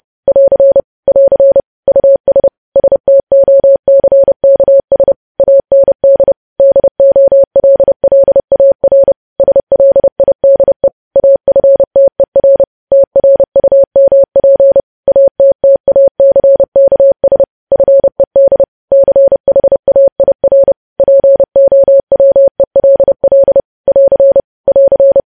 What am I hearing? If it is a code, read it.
Morse code, heard as a keyed tone, about 30 wpm, 550 Hz. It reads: <AR> <AR> US STOCKS AND DOLLAR SLIDE AFTER TRUMP ATTACKS FED CHAIR POWELL <AR> <AR>
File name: podcast / News headlines-1x-30x30